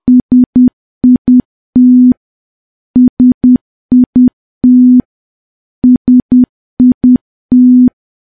Morse code, heard as a keyed tone, about 10 wpm, 250 Hz. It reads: SIT SIT SIT
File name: morse / morse-training-10WPM